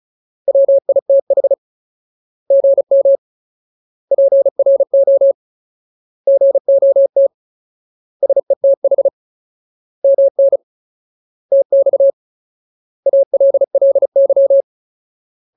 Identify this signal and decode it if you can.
Morse code, heard as a keyed tone, about 34 wpm, 550 Hz. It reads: WITH GM PRO GOT SETH MN TX ALLY
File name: Ultimate Word List / 35WPM Ultimate